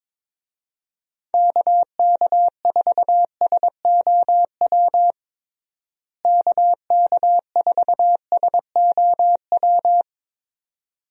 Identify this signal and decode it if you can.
Morse code, heard as a keyed tone, about 22 wpm, 700 Hz. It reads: KK4SOW KK4SOW
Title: USA Call Signs - 6 Characters - Repeated 2x - 22wpm